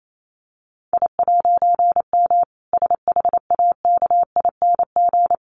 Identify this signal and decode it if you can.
Morse code, heard as a keyed tone, about 28 wpm, 700 Hz. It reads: I'M SHAKING
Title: A Sound Of Thunder 28wpm